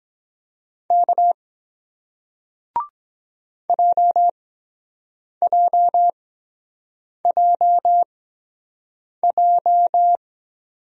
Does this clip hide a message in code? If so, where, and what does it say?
Yes: Morse code from 0.9 to 10.2 s, heard as a keyed tone, about 22 wpm, spaced as for 7 wpm, 700 Hz. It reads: K JJJJ